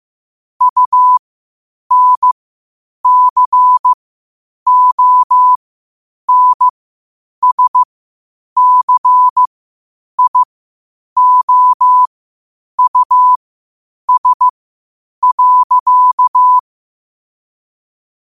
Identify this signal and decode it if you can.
Morse code, heard as a keyed tone, about 15 wpm, spaced as for 8 wpm, 1000 Hz. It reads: UNCONSCIOUS.
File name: qotdfiles / 8wpm